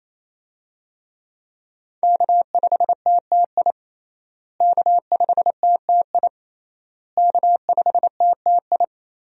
Morse code, heard as a keyed tone, about 28 wpm, 700 Hz. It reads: K5TTI K5TTI K5TTI
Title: USA Call Signs - 5 Characters - Repeated 3x - 28wpm